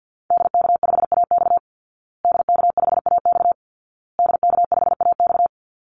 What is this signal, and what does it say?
Morse code, heard as a keyed tone, about 50 wpm, 700 Hz. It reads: DK5AX DK5AX DK5AX
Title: Call Signs Repeated 2x - 50wpm